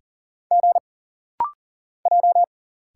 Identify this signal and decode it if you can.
Morse code, heard as a keyed tone, about 40 wpm, 700 Hz. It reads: G J
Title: Single Letters - Rapid-Fire 40wpm